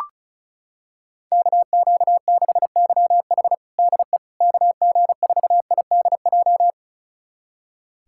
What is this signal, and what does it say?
Morse code, heard as a keyed tone, about 36 wpm, 700 Hz. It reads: KQ6YH DE KG4IDJ
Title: USA Callsign DE Callsign 35wpm